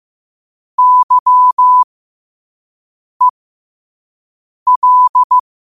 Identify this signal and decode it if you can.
Morse code, heard as a keyed tone, about 15 wpm, spaced as for 5 wpm, 1000 Hz. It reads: YEL
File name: qotdfiles / 5wpm